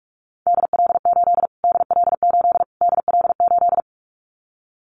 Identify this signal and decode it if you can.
Morse code, heard as a keyed tone, about 44 wpm, 700 Hz. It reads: DL8 DL8 DL8
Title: Callsign Prefixes for Top 25 DXCC - Repeated 3x - 45wpm